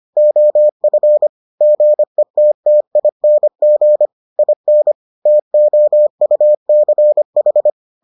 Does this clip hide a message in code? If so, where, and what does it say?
Yes: Morse code from 0.2 to 7.7 s, heard as a keyed tone, about 25 wpm, 600 Hz. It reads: OF GETTING IN TOUCH